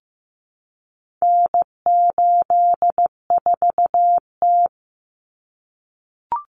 Morse code, heard as a keyed tone, about 15 wpm, 700 Hz. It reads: N84T
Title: Common Cut Numbers - 4-Digit 15wpm